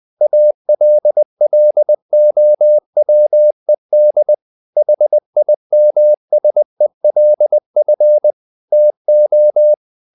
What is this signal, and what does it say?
Morse code, heard as a keyed tone, about 20 wpm, 600 Hz. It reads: ALLOWED HIMSELF TO
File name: the old man and the sea part 2 20wpm